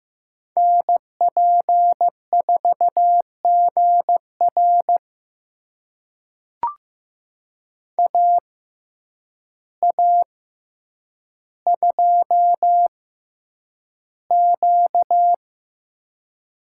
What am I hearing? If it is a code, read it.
Morse code, heard as a keyed tone, about 15 wpm, spaced as for 8 wpm, 700 Hz. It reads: NP4GR AA2Q
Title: USA Call Signs - 5 Characters - 6x Character Spacing - 15wpm